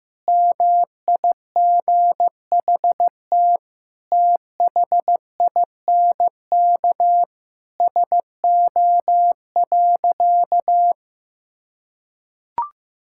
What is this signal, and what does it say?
Morse code, heard as a keyed tone, about 16 wpm, 700 Hz. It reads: MIGHT THINK SO.
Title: Sentences from Top 1000 Words - Review Part 5 - 15wpm